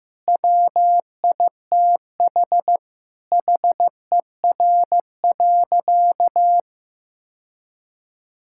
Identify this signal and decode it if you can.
Morse code, heard as a keyed tone, about 14 wpm, 700 Hz. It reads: WITH HER.
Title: Sentences from Top 1600 Words 15wpm